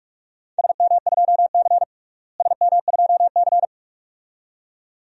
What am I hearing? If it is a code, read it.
Morse code, heard as a keyed tone, about 42 wpm, 700 Hz. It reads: SM2C SM2C
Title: DXCC-100 Call Signs - 4 Characters - Repeated 2x - 45wpm